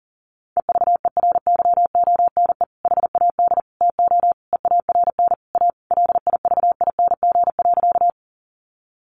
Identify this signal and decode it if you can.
Morse code, heard as a keyed tone, about 40 wpm, 700 Hz. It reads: EVERYONE HAD TO EARN A LIVING.
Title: Sentences from Top 1500 Words 40wpm